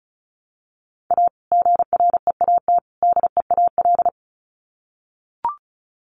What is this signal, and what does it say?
Morse code, heard as a keyed tone, about 34 wpm, 700 Hz. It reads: A GREAT DEAL
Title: Sets of 3 Words 35wpm